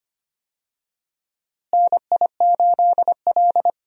Morse code, heard as a keyed tone, about 25 wpm, 700 Hz. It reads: NI8L
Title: USA Call Signs - 4 Characters - Repeated 3x - 25wpm